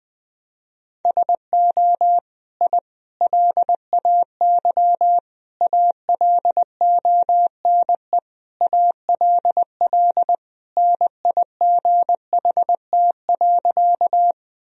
Morse code, heard as a keyed tone, about 20 wpm, 700 Hz. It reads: SO I LAY ALONE ALL NIGHT.
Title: Sentences from Top 1000 Words - Review Part 2 - 20wpm